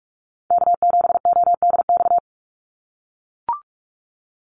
Morse code, heard as a keyed tone, about 44 wpm, 700 Hz. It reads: K7ODX
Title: NA SOTA Activator Callsigns 45wpm